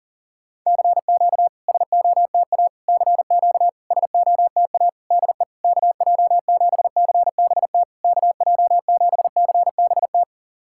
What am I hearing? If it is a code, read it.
Morse code, heard as a keyed tone, about 40 wpm, 690 Hz. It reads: CQ SOTA CQ SOTA DE KJ7CBT KJ7CBT
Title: SOTA QSO Practice 40wpm